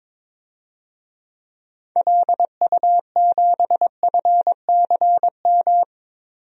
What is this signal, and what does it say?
Morse code, heard as a keyed tone, about 22 wpm, 700 Hz. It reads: LU7FCM
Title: DXCC-100 Call Signs - 6 Characters - Repeated 2x - 22wpm